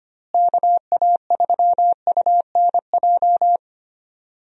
Morse code, heard as a keyed tone, about 25 wpm, 700 Hz. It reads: KA3UNJ
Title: USA Call Signs - 6 Characters - Repeated 3x - 25wpm